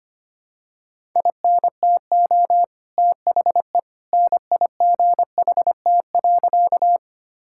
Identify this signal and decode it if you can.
Morse code, heard as a keyed tone, about 25 wpm, 700 Hz. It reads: INTO THE NIGHT.
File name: practice / 25wpm / Sentences with 2-Word Phrases - 25wpm